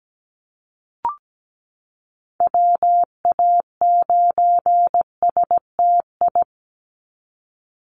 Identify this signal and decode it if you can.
Morse code, heard as a keyed tone, about 17 wpm, 700 Hz. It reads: WA9STI
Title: NA SOTA Activator Callsigns 17wpm